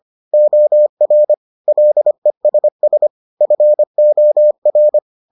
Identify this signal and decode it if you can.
Morse code, heard as a keyed tone, about 25 wpm, 600 Hz. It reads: OR LESS FOR